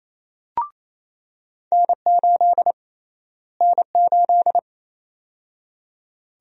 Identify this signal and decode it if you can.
Morse code, heard as a keyed tone, about 28 wpm, 700 Hz. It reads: N8 N8
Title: USA Call Sign Prefixes - Repeated 2x - 28wpm